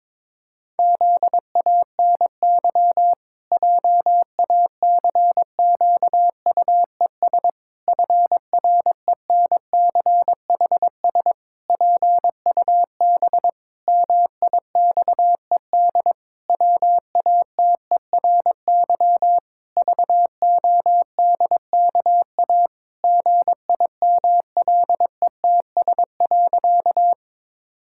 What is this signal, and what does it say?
Morse code, heard as a keyed tone, about 22 wpm, 700 Hz. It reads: ZANY JACQUES FRENCHS PUB MIXED WATERY VODKA GIMLETS.
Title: Pangram Challenge v2 - 22wpm